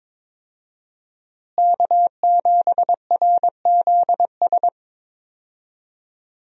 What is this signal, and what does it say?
Morse code, heard as a keyed tone, about 22 wpm, 700 Hz. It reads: K7RZS
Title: USA Call Signs - 5 Characters - Repeated 3x - 22wpm